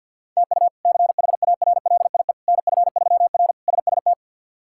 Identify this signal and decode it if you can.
Morse code, heard as a keyed tone, about 50 wpm, 700 Hz. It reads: TU CHARLIE NF3R SST
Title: K1USN SST QSO Practice - 50wpm